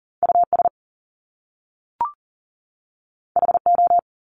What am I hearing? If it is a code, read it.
Morse code, heard as a keyed tone, about 40 wpm, 700 Hz. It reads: US HO
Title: Top Two Letter Combinations in Words 40wpm